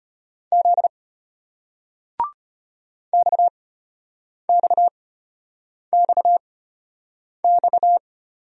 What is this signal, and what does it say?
Morse code, heard as a keyed tone, about 35 wpm, spaced as for 8 wpm, 700 Hz. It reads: Z XXXX